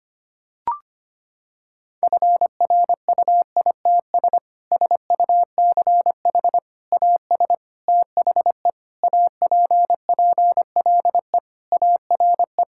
Morse code, heard as a keyed tone, about 25 wpm, 700 Hz. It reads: FRUITS SUCH AS THE APPLE ARE
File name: practice / 25wpm / Sentences with 3-Word Phrases - 25wpm